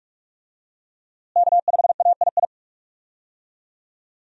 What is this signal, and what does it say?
Morse code, heard as a keyed tone, about 47 wpm, 700 Hz. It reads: K5AII
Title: USA Call Signs - 5 Characters - Repeated 2x - 45wpm